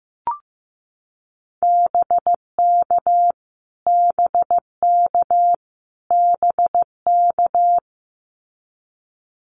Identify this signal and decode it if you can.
Morse code, heard as a keyed tone, about 15 wpm, 700 Hz. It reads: BK BK BK
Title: Abbreviations Repeated 3x - 15wpm